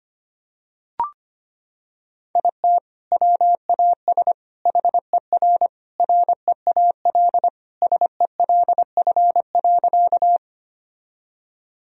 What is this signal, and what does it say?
Morse code, heard as a keyed tone, about 25 wpm, 700 Hz. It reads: IT WAS HER REAL SELF.
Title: Sentences from Top 1000 Words - Review Part 1 - 25wpm